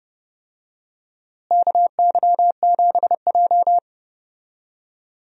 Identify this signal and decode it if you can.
Morse code, heard as a keyed tone, about 30 wpm, 700 Hz. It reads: KY7J
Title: USA Call Signs - 4 Characters - Repeated 3x - 30wpm